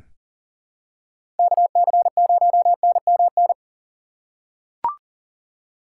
Morse code, heard as a keyed tone, about 37 wpm, 700 Hz. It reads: KC0NMN
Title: USA Call Signs - 3x Character Spacing - 40wpm